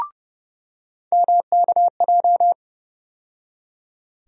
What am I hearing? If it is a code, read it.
Morse code, heard as a keyed tone, about 30 wpm, 700 Hz. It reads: MKJ
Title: USA Call Sign Postfixes - 3 Letters 30wpm